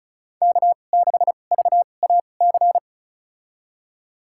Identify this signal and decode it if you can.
Morse code, heard as a keyed tone, about 36 wpm, spaced as for 25 wpm, 700 Hz. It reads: K6VAC
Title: USA Call Signs - 2x Character Spacing - 35wpm